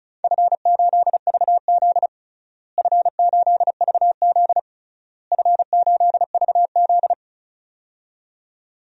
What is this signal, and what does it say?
Morse code, heard as a keyed tone, about 36 wpm, 700 Hz. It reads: F8VZ F8VZ F8VZ
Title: DXCC-100 Call Signs - 4 Characters - Repeated 3x - 35wpm